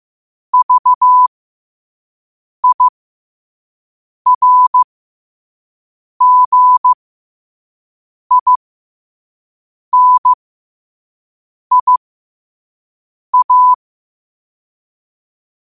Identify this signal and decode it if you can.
Morse code, heard as a keyed tone, about 15 wpm, spaced as for 5 wpm, 1000 Hz. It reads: VIRGINIA
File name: qotdfiles / 5wpm